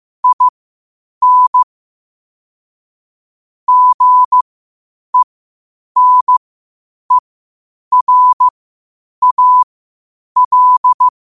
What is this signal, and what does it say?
Morse code, heard as a keyed tone, about 15 wpm, spaced as for 8 wpm, 1000 Hz. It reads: IN GENERAL
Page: New quotes every day in morse code at 8 Words per minute.